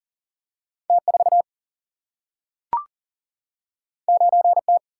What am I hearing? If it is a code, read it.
Morse code, heard as a keyed tone, about 40 wpm, 700 Hz. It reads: T4 9T
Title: Single-Letter Single-Number 40wpm